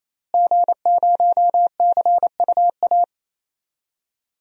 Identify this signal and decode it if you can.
Morse code, heard as a keyed tone, about 28 wpm, 700 Hz. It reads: G0CUA
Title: DXCC-100 Call Signs - 5 Characters - Repeated 3x - 28wpm